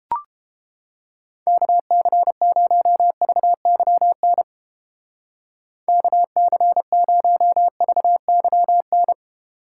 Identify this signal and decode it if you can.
Morse code, heard as a keyed tone, about 30 wpm, 700 Hz. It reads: KC0VYN KC0VYN